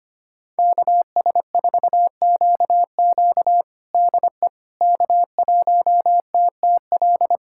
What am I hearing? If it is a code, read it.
Morse code, heard as a keyed tone, about 25 wpm, 700 Hz. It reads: KS4QQ DE K1TTL
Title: USA Callsign DE Callsign - 5 Characters - 25wpm